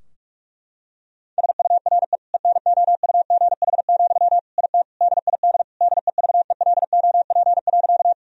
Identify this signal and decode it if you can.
Morse code, heard as a keyed tone, about 46 wpm, 700 Hz. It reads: SURE ENOUGH, IT DID DEVELOP.
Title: Sentences from Top 1000 Words - Review Part 2 - 45wpm